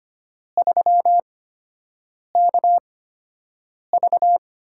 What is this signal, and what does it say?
Morse code, heard as a keyed tone, about 25 wpm, spaced as for 7 wpm, 700 Hz. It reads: 3KV
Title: USA Call Signs - 8x Character Spacing - 25wpm